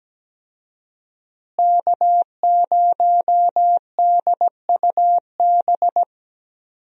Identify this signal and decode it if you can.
Morse code, heard as a keyed tone, about 17 wpm, 700 Hz. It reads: K0DUB
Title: USA Call Signs - 5 Characters - Repeated 2x - 17wpm